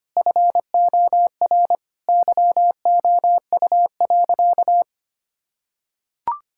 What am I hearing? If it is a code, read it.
Morse code, heard as a keyed tone, about 25 wpm, 700 Hz. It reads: FOR YOU.
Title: Sentences from Top 1000 Words - Review Part 6 - 25wpm